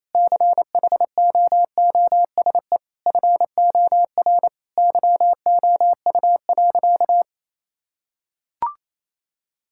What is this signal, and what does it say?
Morse code, heard as a keyed tone, about 27 wpm, 700 Hz. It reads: CHOOSE FOR YOU.
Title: Sentences from Top 1000 Words - Review Part 6 - 28wpm